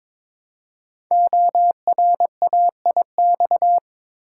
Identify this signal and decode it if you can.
Morse code, heard as a keyed tone, about 22 wpm, 700 Hz. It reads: ORAIX
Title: Code Groups - 5 Letters - 22wpm